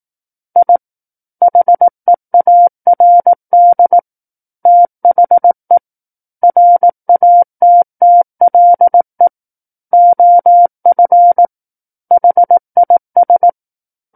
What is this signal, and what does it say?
Morse code, heard as a keyed tone, about 18 wpm, 700 Hz. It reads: I HEARD THE RATTLE OF HIS
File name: 18 WPM